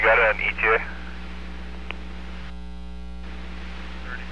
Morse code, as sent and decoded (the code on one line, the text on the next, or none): none